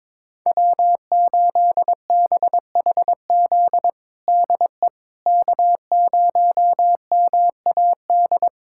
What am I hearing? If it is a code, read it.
Morse code, heard as a keyed tone, about 21 wpm, 700 Hz. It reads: W8BHZ DE K0MAD
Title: USA Callsign DE Callsign - 5 Characters - 22wpm